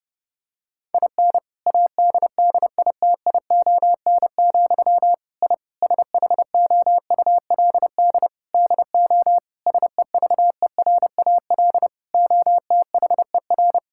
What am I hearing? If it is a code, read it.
Morse code, heard as a keyed tone, about 32 wpm, 700 Hz. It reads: IN ADDITION, I SHOULD DO SEVERAL OTHER
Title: Sentences from Top 1000 Words - Review Part 6 - 30wpm